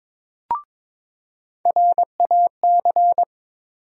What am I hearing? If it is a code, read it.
Morse code, heard as a keyed tone, about 23 wpm, 700 Hz. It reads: RAC
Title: USA Call Sign Postfixes - 3 Letters 22wpm